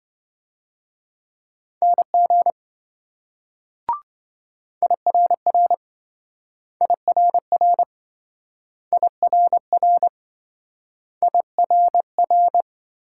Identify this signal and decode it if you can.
Morse code, heard as a keyed tone, about 26 wpm, 700 Hz. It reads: NG IRR IRR IRR IRR